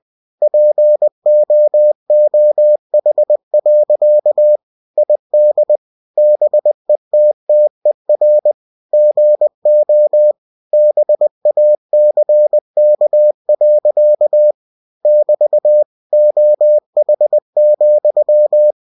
Morse code, heard as a keyed tone, about 20 wpm, 600 Hz. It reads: POOH. ID BETTER GO BACK. <BT> OH,
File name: cw-books / winnie-the-pooh / winnie-20wpm-page-0016